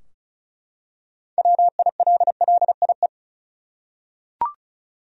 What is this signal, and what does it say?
Morse code, heard as a keyed tone, about 36 wpm, 700 Hz. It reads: WILLIE
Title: USA Common Names - 35wpm